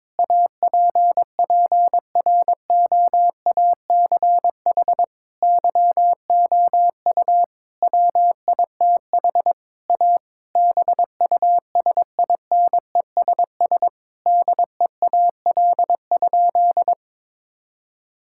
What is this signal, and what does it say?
Morse code, heard as a keyed tone, about 22 wpm, 700 Hz. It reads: APPROACH YOU WITH A BUSINESS DEAL?
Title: Sentences from Top 1000 Words - Review Part 6 - 22wpm